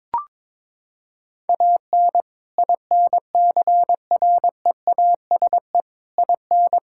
Morse code, heard as a keyed tone, about 22 wpm, 700 Hz. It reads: AN INCREASE IN
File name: practice / 22wpm / Sets of 6 Words - 22wpm